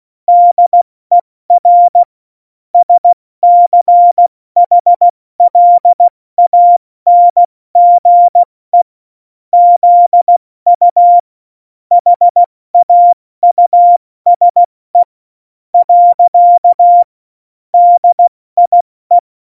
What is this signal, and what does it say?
Morse code, heard as a keyed tone, about 16 wpm, spaced as for 14 wpm, 700 Hz. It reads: DER SCHLANGE ZU HAUSE. DIE
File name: Morsekurs / Witze / Witz 27 March 2018, 70 F80